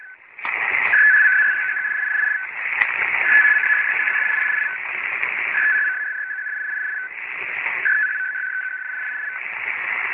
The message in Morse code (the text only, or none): none